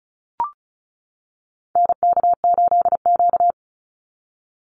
Morse code, heard as a keyed tone, about 34 wpm, 700 Hz. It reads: NK8Q
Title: NA SOTA Activator Callsigns 35wpm